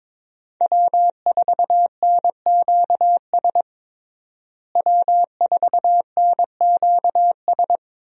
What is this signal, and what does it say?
Morse code, heard as a keyed tone, about 22 wpm, 700 Hz. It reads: W4NQS W4NQS
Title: USA Call Signs - 5 Characters - Repeated 2x - 22wpm